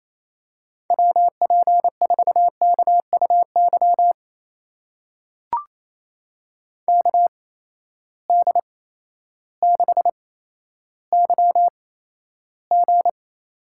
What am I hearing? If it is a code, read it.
Morse code, heard as a keyed tone, about 28 wpm, spaced as for 13 wpm, 700 Hz. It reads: WP4KUY KD6YG
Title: USA Call Signs - 8x Character Spacing - 28wpm